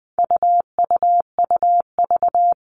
Morse code, heard as a keyed tone, about 20 wpm, 700 Hz. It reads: UUUV
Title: All Cut Numbers - 4-Digit 20wpm